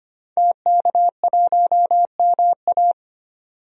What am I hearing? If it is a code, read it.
Morse code, heard as a keyed tone, about 25 wpm, 700 Hz. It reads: TK1MA
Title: DXCC-100 Call Signs - 5 Characters - 25wpm